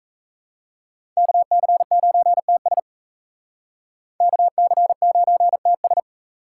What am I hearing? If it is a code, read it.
Morse code, heard as a keyed tone, about 35 wpm, 700 Hz. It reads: KC9TS KC9TS